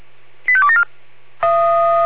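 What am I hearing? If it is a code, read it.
Five-tone selective call (ZVEI), heard as quick runs of five tones, tones of about 70 ms. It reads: 85274
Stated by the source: ZVEI five-tone call received by SDR.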